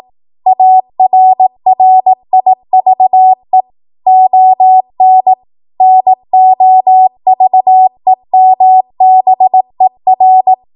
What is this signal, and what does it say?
Morse code, heard as a keyed tone, about 18 wpm, 750 Hz. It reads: ARRIVE ON NOVEMBER
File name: Archive / 18 WPM / 18WPM